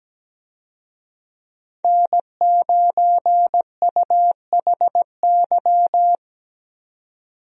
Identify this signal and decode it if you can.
Morse code, heard as a keyed tone, about 17 wpm, 700 Hz. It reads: N9UHY
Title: USA Call Signs - 5 Characters - 17wpm